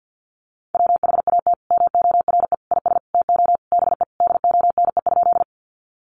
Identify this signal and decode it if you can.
Morse code, heard as a keyed tone, about 50 wpm, 700 Hz. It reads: WHAT MORE IS TO BE DONE?
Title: Sentences from Top 500 Words 50wpm